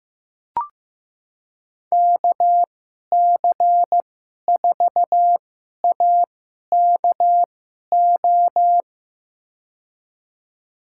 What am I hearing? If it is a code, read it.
Morse code, heard as a keyed tone, about 15 wpm, spaced as for 11 wpm, 700 Hz. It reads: KC4AKO
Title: USA Call Signs - 6 Characters - 2x Character Spacing - 15wpm